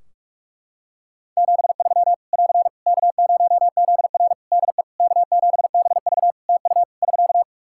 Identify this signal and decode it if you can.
Morse code, heard as a keyed tone, about 46 wpm, 700 Hz. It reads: 73 <AR> K0ZR DE K7BV TU <SK>
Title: K1USN SST QSO Practice 45wpm